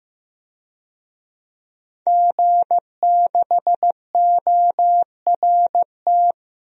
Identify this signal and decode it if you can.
Morse code, heard as a keyed tone, about 15 wpm, 700 Hz. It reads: G6ORT
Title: DXCC-100 Call Signs - 5 Characters - 15wpm